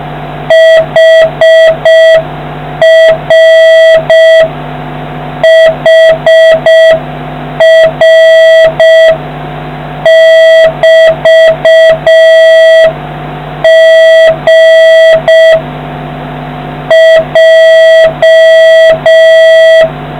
Morse code, heard as a keyed tone, about 6 wpm, 650 Hz, with noise. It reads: HR HR<BT>G J